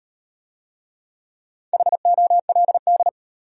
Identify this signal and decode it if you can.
Morse code, heard as a keyed tone, about 33 wpm, 700 Hz. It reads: HOLD